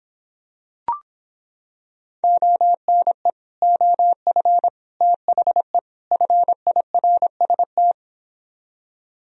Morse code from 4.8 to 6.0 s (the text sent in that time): THE